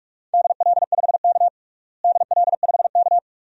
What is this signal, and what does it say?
Morse code, heard as a keyed tone, about 41 wpm, 700 Hz. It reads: DL5K DL5K
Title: DXCC-100 Call Signs - 4 Characters - Repeated 2x - 45wpm